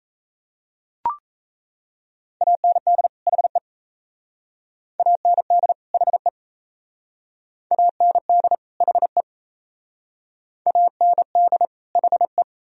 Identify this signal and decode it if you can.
Morse code, heard as a keyed tone, about 34 wpm, 700 Hz. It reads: AND HE AND HE AND HE AND HE
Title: Speed-Racing - Sets of 2 Words 28wpm